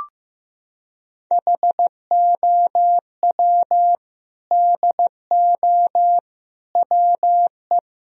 Morse code, heard as a keyed tone, about 15 wpm, 700 Hz. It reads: HOW DO WE
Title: Sentences from Top 1000 Words - Review Part 3 - 15wpm